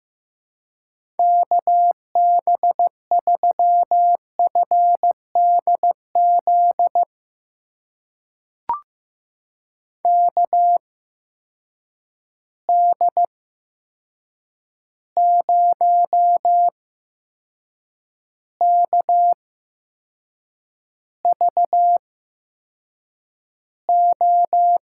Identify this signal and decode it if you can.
Morse code, heard as a keyed tone, about 15 wpm, spaced as for 6 wpm, 700 Hz. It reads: KB3FDZ KD0KVO
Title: USA Call Signs - 6 Characters - 8x Character Spacing - 15wpm